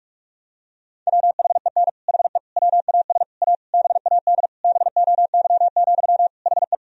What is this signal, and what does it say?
Morse code, heard as a keyed tone, about 41 wpm, 700 Hz. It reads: WHEN HE WAS A BAD BOY, HE
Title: Sentences from Top 1000 Words - Review Part 3 - 45wpm